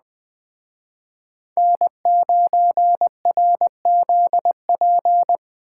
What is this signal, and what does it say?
Morse code, heard as a keyed tone, about 20 wpm, 700 Hz. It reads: N9RZP
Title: Speed Racing - USA 1x3 Call Signs 17wpm